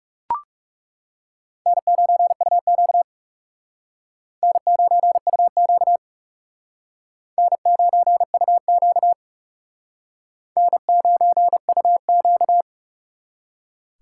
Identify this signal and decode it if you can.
Morse code, heard as a keyed tone, about 35 wpm, 700 Hz. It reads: N9UQ N9UQ N9UQ N9UQ
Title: Speed Racing - USA 1x2 Call Signs 30wpm